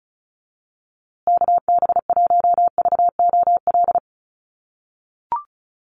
Morse code, heard as a keyed tone, about 34 wpm, 700 Hz. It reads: KB1VOL
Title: USA Call Signs Repeated 3x 35wpm